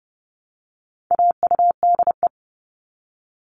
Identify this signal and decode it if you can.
Morse code, heard as a keyed tone, about 30 wpm, 700 Hz. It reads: AUDE
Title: All Cut Numbers - 4-Digit 30wpm